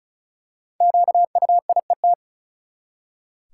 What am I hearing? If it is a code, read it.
Morse code, heard as a keyed tone, about 36 wpm, 700 Hz. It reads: QUIET
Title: Five Letter Words - 35wpm